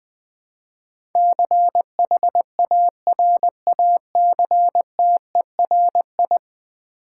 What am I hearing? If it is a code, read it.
Morse code, heard as a keyed tone, about 20 wpm, 700 Hz. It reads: CHARACTERI
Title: Build A Word - 5 Letters At A Time - 12-Letter Words - 20wpm